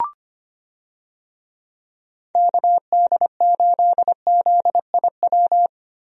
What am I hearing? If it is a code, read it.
Morse code, heard as a keyed tone, about 25 wpm, 700 Hz. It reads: KD8ZIW
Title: USA Call Signs - 6 Characters - Repeated 2x - 25wpm